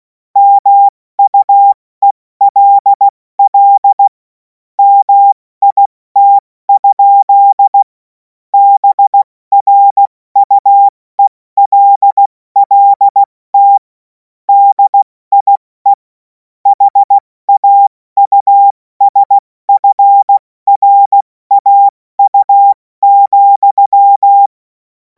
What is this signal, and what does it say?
Morse code, heard as a keyed tone, about 16 wpm, 800 Hz. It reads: MUELL MIT? BRUELLT DIE HAUSFRAU,
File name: Morsekurs / Witze / Witz 23. Juni 2026 70 F80